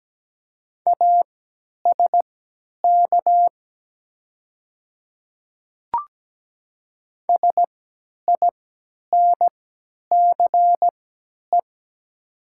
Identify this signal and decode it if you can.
Morse code, heard as a keyed tone, about 17 wpm, spaced as for 10 wpm, 700 Hz. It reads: ASK SINCE
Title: 3x Character Spacing - Top 500 Words - 17wpm